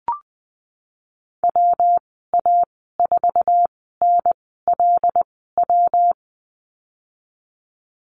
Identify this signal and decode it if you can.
Morse code, heard as a keyed tone, about 20 wpm, spaced as for 15 wpm, 700 Hz. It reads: WA4NLW
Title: 167 Lesson 27g - Number 7 - Call Signs 20wpm